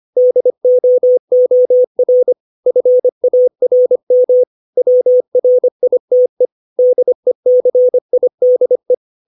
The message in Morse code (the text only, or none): DOOR FARM WRITE DECIDE